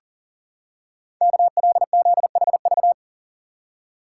Morse code, heard as a keyed tone, about 40 wpm, 700 Hz. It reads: KLZHV
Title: Code Groups - 5 Letters - 40wpm